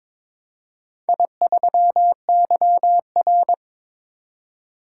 Morse code, heard as a keyed tone, about 22 wpm, 700 Hz. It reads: I3YR
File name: practice / 22wpm / Four Letter-Number Combinations - 22wpm